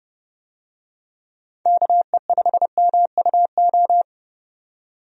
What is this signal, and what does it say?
Morse code, heard as a keyed tone, about 29 wpm, 700 Hz. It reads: KE5MUO
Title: USA Call Signs - 6 Characters - 30wpm